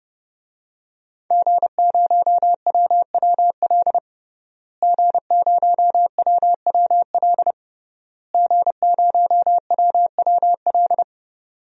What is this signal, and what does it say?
Morse code, heard as a keyed tone, about 32 wpm, 700 Hz. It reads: G0WWL G0WWL G0WWL
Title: DXCC-100 Call Signs - 5 Characters - Repeated 3x - 30wpm